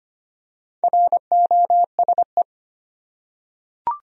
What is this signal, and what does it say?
Morse code, heard as a keyed tone, about 25 wpm, 700 Hz. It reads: ROSE
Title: USA Common Names - Max 5 Letters - 25wpm